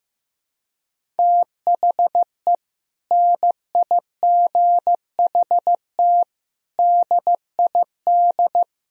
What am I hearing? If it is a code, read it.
Morse code, heard as a keyed tone, about 15 wpm, 700 Hz. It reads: THE NIGHT DID
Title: Sentences from Top 1000 Words - Review Part 1 - 15wpm